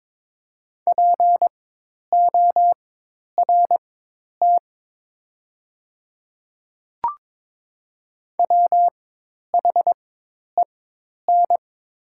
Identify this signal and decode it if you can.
Morse code, heard as a keyed tone, about 22 wpm, spaced as for 10 wpm, 700 Hz. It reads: PORT WHEN
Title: 4x Character Spacing - Top 200 Words - 22wpm